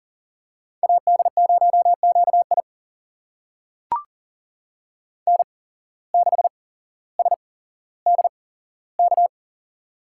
Morse code, heard as a keyed tone, about 40 wpm, spaced as for 17 wpm, 700 Hz. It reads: AD0QI N6SDK
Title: USA Call Signs - 5 Characters - 8x Character Spacing - 40wpm